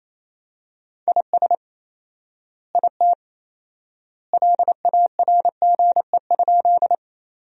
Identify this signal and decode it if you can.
Morse code, heard as a keyed tone, about 27 wpm, 700 Hz. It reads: IS IT LARGE?
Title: 4x Word-Spacing - Sentences from Top 200 Words - 28wpm